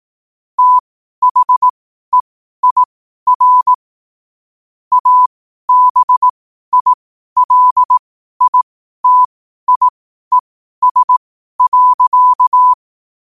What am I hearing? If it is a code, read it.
Morse code, heard as a keyed tone, about 18 wpm, spaced as for 12 wpm, 1000 Hz. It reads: THEIR ABILITIES.
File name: qotdfiles / 12wpm